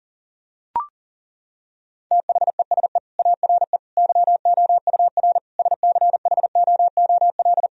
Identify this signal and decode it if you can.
Morse code, heard as a keyed tone, about 37 wpm, 700 Hz. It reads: THESE ARE YOUR SCHOOL
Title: Sentences from Top 1000 Words - Review Part 1 - 40wpm